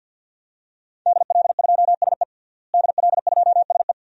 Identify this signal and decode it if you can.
Morse code, heard as a keyed tone, about 52 wpm, 700 Hz. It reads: DL2SE DL2SE
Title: DXCC-100 Call Signs - 5 Characters - Repeated 2x - 50wpm